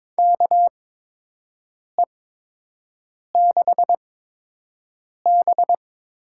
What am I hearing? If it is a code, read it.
Morse code, heard as a keyed tone, about 22 wpm, spaced as for 6 wpm, 700 Hz. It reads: KE6B
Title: USA Call Signs - 6 Characters - 8x Character Spacing - 22wpm